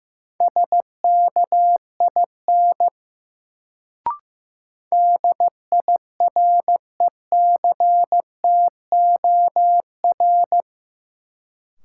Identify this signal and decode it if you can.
Morse code, heard as a keyed tone, about 15 wpm, 700 Hz. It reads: SKIN DIRECTOR
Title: Sentences from Top 1300 Words - New Word Drill - 15wpm